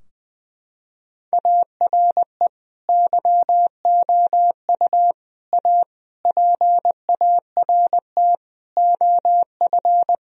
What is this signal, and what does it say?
Morse code, heard as a keyed tone, about 20 wpm, 700 Hz. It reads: ARE YOU A PART OF
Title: Sentences from Top 1000 Words - Review Part 4 - 20wpm